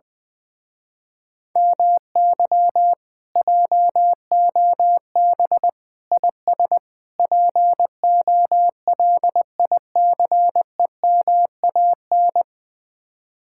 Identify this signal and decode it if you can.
Morse code, heard as a keyed tone, about 20 wpm, 700 Hz. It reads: MY JOB IS POLICEMAN
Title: Speed-Racing Common QSO Phrases - 20wpm